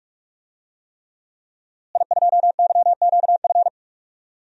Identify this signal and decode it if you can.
Morse code, heard as a keyed tone, about 44 wpm, 700 Hz. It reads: I2YQF
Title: DXCC-100 Call Signs - 5 Characters - Repeated 3x - 45wpm